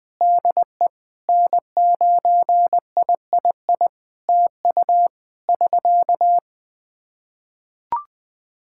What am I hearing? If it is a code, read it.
Morse code, heard as a keyed tone, about 20 wpm, 700 Hz. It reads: DE N9III TU <SK>
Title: Practice QSO - Exchange 4 - Ending - 20wpm